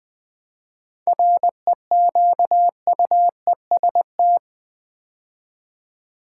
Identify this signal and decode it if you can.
Morse code, heard as a keyed tone, about 20 wpm, 700 Hz. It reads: REQUEST
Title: 7 Letter Words - First 3 Letters Given - 20wpm